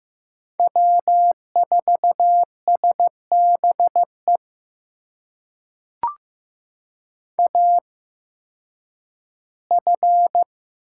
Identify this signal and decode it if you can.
Morse code, heard as a keyed tone, about 15 wpm, spaced as for 10 wpm, 700 Hz. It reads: W4SBE AF
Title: USA Call Signs - 5 Characters - 8x Character Spacing - 15wpm